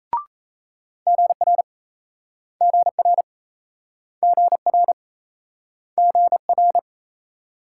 Morse code, heard as a keyed tone, about 36 wpm, 700 Hz. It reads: GR GR GR GR